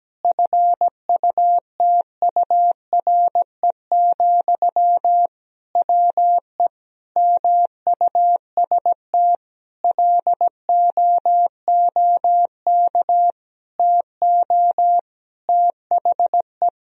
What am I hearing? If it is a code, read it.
Morse code, heard as a keyed tone, about 17 wpm, 700 Hz. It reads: FUTURE, WE MUST LOOK TO THE
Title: Sentences from Top 1000 Words - Review Part 3 - 17wpm